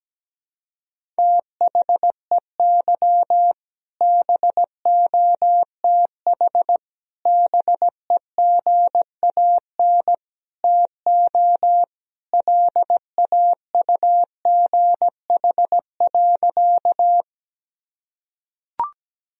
Morse code, heard as a keyed tone, about 17 wpm, 700 Hz. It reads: THEY BOTH BEGAN TO LAUGH.
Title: Sentences from Top 1000 Words - Review Part 1 - 17wpm